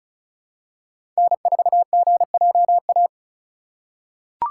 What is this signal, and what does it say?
Morse code, heard as a keyed tone, about 34 wpm, 700 Hz. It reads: N4GJA
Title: USA Call Signs - 5 Characters - 2x Character Spacing - 35wpm